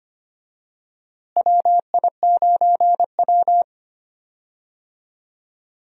Morse code, heard as a keyed tone, about 25 wpm, 700 Hz. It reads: WI9W
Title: USA Call Signs - 4 Characters - Repeated 3x - 25wpm